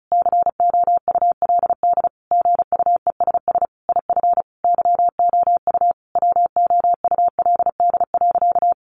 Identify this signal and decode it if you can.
Morse code, heard as a keyed tone, about 34 wpm, 700 Hz. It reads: COULD GUESS IF YOU WOULD.
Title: Sentences from Top 1500 Words 35wpm